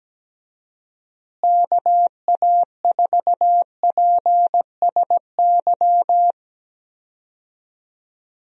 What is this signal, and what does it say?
Morse code, heard as a keyed tone, about 17 wpm, 700 Hz. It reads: KA4PSY
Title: USA Call Signs - 6 Characters - Repeated 2x - 17wpm